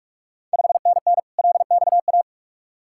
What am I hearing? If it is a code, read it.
Morse code, heard as a keyed tone, about 46 wpm, 700 Hz. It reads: 5NN LXA
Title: Ontario QSO Party Exchange 45wpm